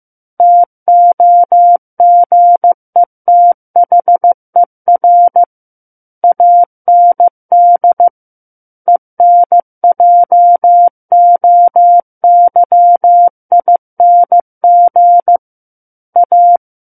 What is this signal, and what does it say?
Morse code, heard as a keyed tone, about 15 wpm, 700 Hz. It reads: TOGETHER AND ENJOYING A
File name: Chapter 6 - 15 WPM